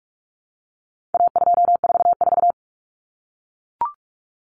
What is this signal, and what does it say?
Morse code, heard as a keyed tone, about 44 wpm, 700 Hz. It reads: A244
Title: Common Cut Numbers - 4-Digit 45wpm